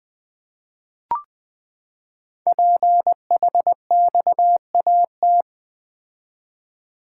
Morse code, heard as a keyed tone, about 20 wpm, 700 Hz. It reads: PHXAT
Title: Code Groups - 5 Letter:Numbers - 20wpm